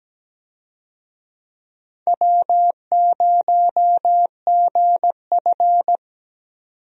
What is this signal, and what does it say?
Morse code, heard as a keyed tone, about 17 wpm, 700 Hz. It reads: W0GF
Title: USA Call Signs - 4 Characters - 17wpm